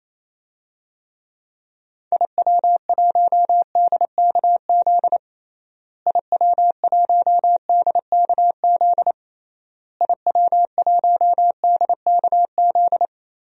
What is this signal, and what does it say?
Morse code, heard as a keyed tone, about 28 wpm, 700 Hz. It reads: IW1DKZ IW1DKZ IW1DKZ
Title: DXCC-100 Call Signs - 6 Characters - Repeated 3x - 28wpm